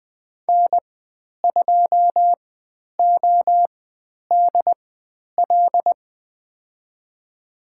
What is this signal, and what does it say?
Morse code, heard as a keyed tone, about 20 wpm, spaced as for 10 wpm, 700 Hz. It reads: N2ODL
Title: 095 Lesson 16g - Letter c - Call Signs 20wpm